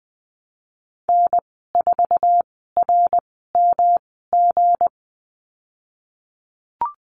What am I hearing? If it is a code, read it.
Morse code, heard as a keyed tone, about 20 wpm, spaced as for 15 wpm, 700 Hz. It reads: N4RMG
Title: 173 Lesson 28g - Number 9 - Call Signs 20wpm